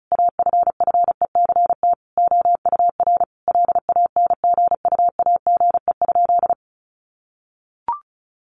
Morse code, heard as a keyed tone, about 34 wpm, 700 Hz. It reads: AFFECT OUR LANGUAGE?
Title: Sentences from Top 900 Words 35wpm